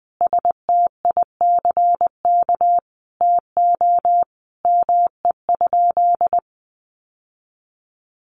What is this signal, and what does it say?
Morse code, heard as a keyed tone, about 20 wpm, 700 Hz. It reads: STICK TO ME?
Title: Sentences from Top 1200 Words 20wpm